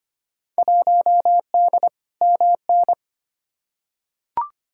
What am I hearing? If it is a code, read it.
Morse code, heard as a keyed tone, about 25 wpm, 700 Hz. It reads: 1D MN
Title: Field Day Exchanges - 25wpm